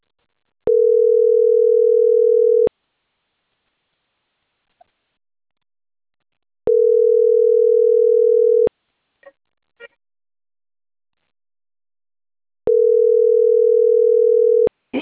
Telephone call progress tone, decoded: ringback tone